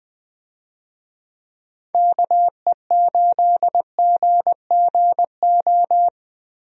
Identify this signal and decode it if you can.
Morse code, heard as a keyed tone, about 20 wpm, 700 Hz. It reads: KE8GGO
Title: USA Call Signs - 6 Characters - Repeated 2x - 20wpm